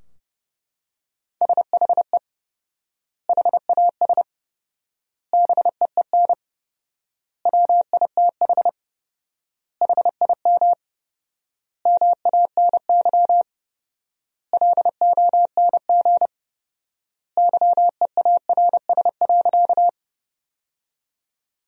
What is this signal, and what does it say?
Morse code, heard as a keyed tone, about 30 wpm, 700 Hz. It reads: SHE HAS BEEN WITH HIM MANY LONG YEARS.
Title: 4x Word-Spacing - Sentences from Top 200 Words - 30wpm